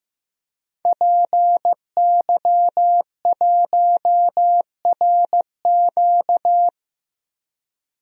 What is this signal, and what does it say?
Morse code, heard as a keyed tone, about 15 wpm, 700 Hz. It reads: PY1RQ
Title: DXCC-100 Call Signs - 5 Characters - 15wpm